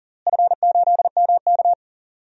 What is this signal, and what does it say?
Morse code, heard as a keyed tone, about 40 wpm, 700 Hz. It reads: F8MK
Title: DXCC-100 Call Signs - 4 Characters - Repeated 2x - 40wpm